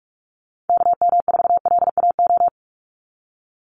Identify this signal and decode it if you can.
Morse code, heard as a keyed tone, about 44 wpm, 700 Hz. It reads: KM4LAO
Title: Call Signs Repeated 3x - 45wpm